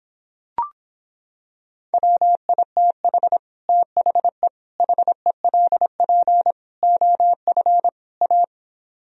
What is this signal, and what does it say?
Morse code, heard as a keyed tone, about 26 wpm, 700 Hz. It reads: WITH THE HELP OF A